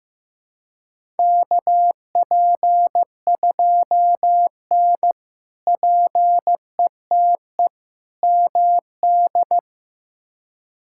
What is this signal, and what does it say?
Morse code, heard as a keyed tone, about 15 wpm, 700 Hz. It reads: KP2N PETE MD
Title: K1USN SST QSO Practice - 15wpm